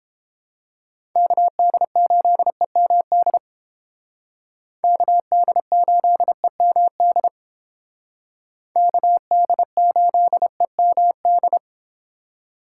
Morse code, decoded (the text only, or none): KD8EMD KD8EMD KD8EMD